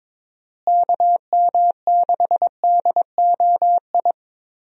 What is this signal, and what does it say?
Morse code, heard as a keyed tone, about 22 wpm, 700 Hz. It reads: KM6DOI
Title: USA Call Signs - 6 Characters - Repeated 3x - 22wpm